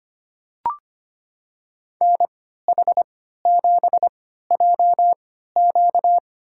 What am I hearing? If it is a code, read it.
Morse code, heard as a keyed tone, about 25 wpm, spaced as for 14 wpm, 700 Hz. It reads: NH7JQ
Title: USA Call Signs - 5 Characters - 3x Character Spacing - 25wpm